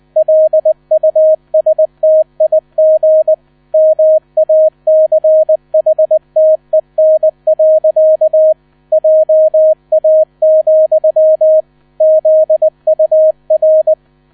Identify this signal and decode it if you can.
Morse code, heard as a keyed tone, about 20 wpm, 630 Hz. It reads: LUSTIG MACHTEN. JA, ZUR
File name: 000 Vorwort 20wpm